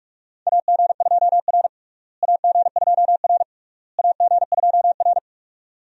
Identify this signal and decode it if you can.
Morse code, heard as a keyed tone, about 44 wpm, 700 Hz. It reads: AG2R AG2R AG2R
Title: USA Call Signs - 4 Characters - Repeated 3x - 45wpm